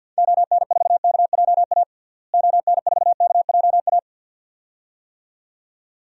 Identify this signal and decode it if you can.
Morse code, heard as a keyed tone, about 52 wpm, 700 Hz. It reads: ON4KJA ON4KJA
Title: DXCC-100 Call Signs - 6 Characters - Repeated 2x - 50wpm